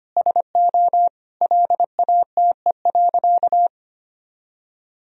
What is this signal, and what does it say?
Morse code, heard as a keyed tone, about 25 wpm, 700 Hz. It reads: SO LATE.
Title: Sentences from Top 1000 Words - Review Part 3 - 25wpm